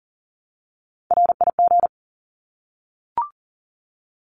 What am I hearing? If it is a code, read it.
Morse code, heard as a keyed tone, about 40 wpm, 700 Hz. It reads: RIG
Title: Abbreviations Repeated 3x - 40wpm